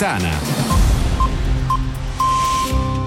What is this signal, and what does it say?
Speech. There is no code.